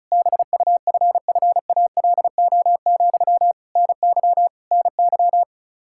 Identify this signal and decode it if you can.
Morse code, heard as a keyed tone, about 36 wpm, 700 Hz. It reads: BUFFALO, NY NY
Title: K1USN SST QSO Practice 35wpm